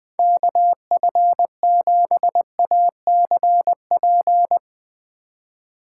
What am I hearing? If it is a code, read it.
Morse code, heard as a keyed tone, about 21 wpm, 700 Hz. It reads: KF7ACP
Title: Speed Racing - USA 2x3 Call Signs 17wpm